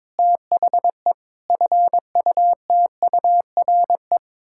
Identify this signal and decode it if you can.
Morse code, heard as a keyed tone, about 22 wpm, 700 Hz. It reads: THE FUTURE
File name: practice / 22wpm / Sets of 6 Words - 22wpm